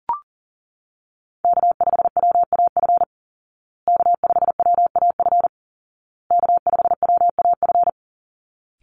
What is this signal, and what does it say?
Morse code, heard as a keyed tone, about 40 wpm, 700 Hz. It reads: K5WAF K5WAF K5WAF
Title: Call Signs Repeated 2x - 40wpm